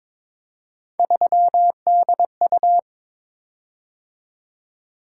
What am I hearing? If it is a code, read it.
Morse code, heard as a keyed tone, about 22 wpm, 700 Hz. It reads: 3DU
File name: practice / 22wpm / Three Letter-Number Combinations - 22wpm